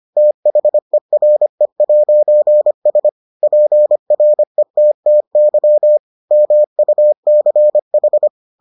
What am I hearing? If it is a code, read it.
Morse code, heard as a keyed tone, about 25 wpm, 600 Hz. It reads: THERE'S PRETTY MUCH